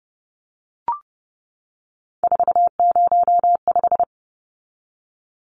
Morse code, heard as a keyed tone, about 30 wpm, 700 Hz. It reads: 405
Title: Numbers - 3-Digit 30wpm